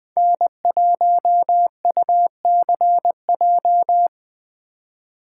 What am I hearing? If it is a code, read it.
Morse code, heard as a keyed tone, about 20 wpm, 700 Hz. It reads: N1UCJ
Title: USA Call Signs - 5 Characters - 8x Character Spacing - 20wpm